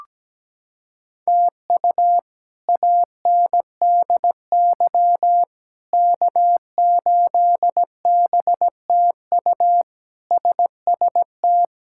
Morse code, heard as a keyed tone, about 17 wpm, 700 Hz. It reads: TU ANDY K8BTU SST
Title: K1USN SST QSO Practice - 17wpm